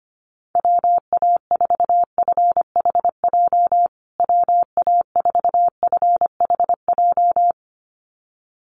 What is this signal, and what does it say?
Morse code, heard as a keyed tone, about 25 wpm, 700 Hz. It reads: WA4FHJ WA4FHJ
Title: USA Call Signs Repeated 2x 25wpm